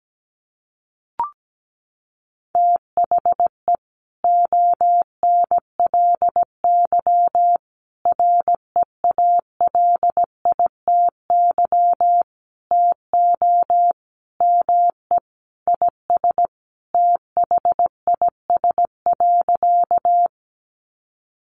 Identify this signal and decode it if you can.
Morse code, heard as a keyed tone, about 17 wpm, 700 Hz. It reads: THE ONLY REALITY TO ME IS THIS.
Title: Sentences from Top 1200 Words 17wpm